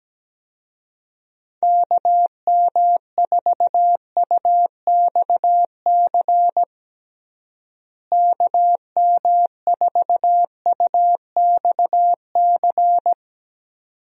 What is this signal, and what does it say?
Morse code, heard as a keyed tone, about 17 wpm, 700 Hz. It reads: KM4UXC KM4UXC
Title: USA Call Signs - 6 Characters - Repeated 2x - 17wpm